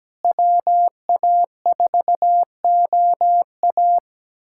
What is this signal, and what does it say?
Morse code, heard as a keyed tone, about 17 wpm, 700 Hz. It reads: WA4OA
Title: USA Callsign DE Callsign - 5 Characters - 17wpm